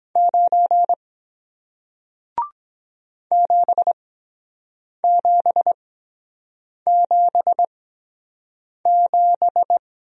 Morse code, heard as a keyed tone, about 24 wpm, spaced as for 7 wpm, 700 Hz. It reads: 9 7777